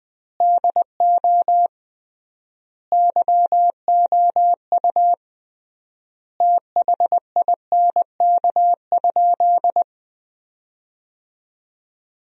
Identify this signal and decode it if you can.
Morse code, heard as a keyed tone, about 20 wpm, 700 Hz. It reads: DO YOU THINK?
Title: 3x Word-Spacing - Sentences from Top 200 Words - 20wpm